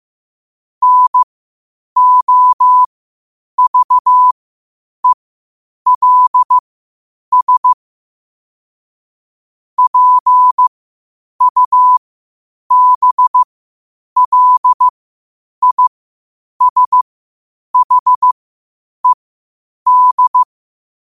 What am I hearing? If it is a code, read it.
Morse code, heard as a keyed tone, about 15 wpm, spaced as for 8 wpm, 1000 Hz. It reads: NOVELS PUBLISHED
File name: qotdfiles / 8wpm